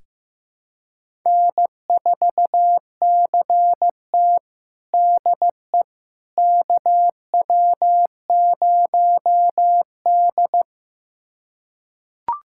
Morse code, heard as a keyed tone, about 15 wpm, 700 Hz. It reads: N4CT DE KW0D
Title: USA Callsign DE Callsign - 4 Characters - 15wpm